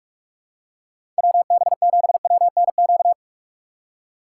Morse code, heard as a keyed tone, about 46 wpm, 700 Hz. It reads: WB7WNQ
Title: USA Call Signs - 6 Characters - 3x Character Spacing - 45wpm